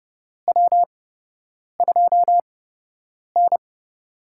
Morse code, heard as a keyed tone, about 30 wpm, spaced as for 8 wpm, 700 Hz. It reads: W2N
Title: USA Call Signs - 8x Character Spacing - 30wpm